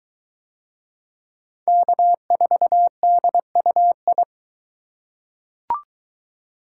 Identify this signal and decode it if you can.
Morse code, heard as a keyed tone, about 20 wpm, 700 Hz. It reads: K4DUI